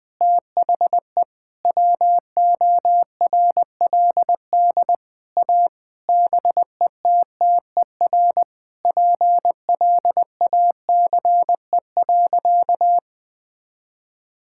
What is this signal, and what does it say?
Morse code, heard as a keyed tone, about 20 wpm, 700 Hz. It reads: THE WORLD A BETTER PLACE.
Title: Sentences with 6-Word Phrases - 20wpm